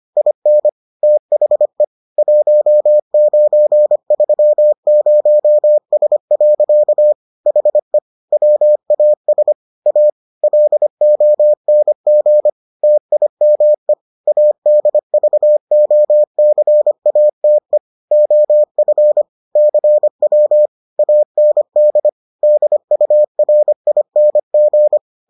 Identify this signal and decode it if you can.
Morse code, heard as a keyed tone, about 25 wpm, 600 Hz. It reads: IN THE 1930S. HE WAS A LONG TIME ADVOCATE OF CW AND DURING